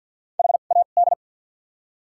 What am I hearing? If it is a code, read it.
Morse code, heard as a keyed tone, about 54 wpm, spaced as for 37 wpm, 700 Hz. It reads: HAD
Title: 2x Character Spacing - Top 100 Words - 50wpm